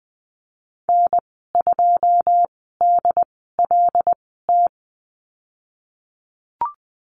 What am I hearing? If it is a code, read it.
Morse code, heard as a keyed tone, about 20 wpm, spaced as for 15 wpm, 700 Hz. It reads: N2DLT
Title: 088 Lesson 15g - Number 5 - Call Signs 20wpm